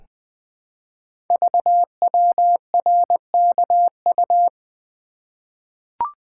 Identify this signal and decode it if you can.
Morse code, heard as a keyed tone, about 20 wpm, 700 Hz. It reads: VWRKU
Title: Code Groups - 5 Letter:Numbers - 20wpm